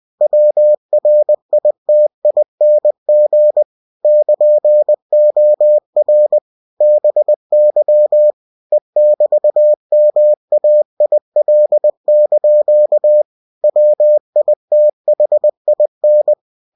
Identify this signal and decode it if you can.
Morse code, heard as a keyed tone, about 20 wpm, 600 Hz. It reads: WRITING <KN>OR BY E<BT>MAIL) WITHIN
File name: cw-books / winnie-the-pooh / winnie-20wpm-page-0117